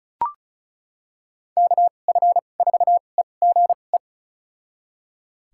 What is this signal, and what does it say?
Morse code, heard as a keyed tone, about 34 wpm, spaced as for 25 wpm, 690 Hz. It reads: KF4EGE
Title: USA Call Signs - 6 Characters - 2x Character Spacing - 35wpm